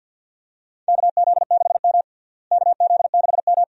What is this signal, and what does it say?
Morse code, heard as a keyed tone, about 52 wpm, 700 Hz. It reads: KZ6M KZ6M
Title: USA Call Signs - 4 Characters - Repeated 2x - 50wpm